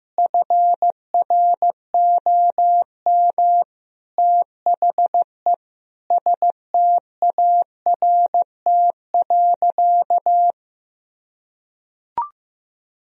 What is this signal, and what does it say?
Morse code, heard as a keyed tone, about 16 wpm, 700 Hz. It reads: FROM THE START.
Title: Sentences from Top 1000 Words - Review Part 1 - 15wpm